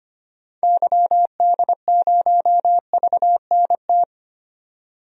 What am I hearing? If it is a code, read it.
Morse code, heard as a keyed tone, about 25 wpm, 700 Hz. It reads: YD0VNT
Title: DXCC-100 Call Signs - 6 Characters - 25wpm